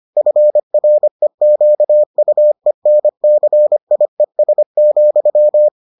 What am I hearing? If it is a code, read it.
Morse code, heard as a keyed tone, about 25 wpm, 600 Hz. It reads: FREQUENCIES,